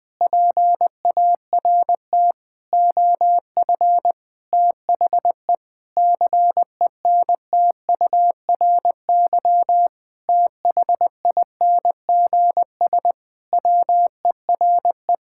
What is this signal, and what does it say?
Morse code, heard as a keyed tone, about 20 wpm, 700 Hz. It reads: PART OF THE CENTURY THINGS WERE
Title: Sentences with 6-Word Phrases - Spoken Beginning - 20wpm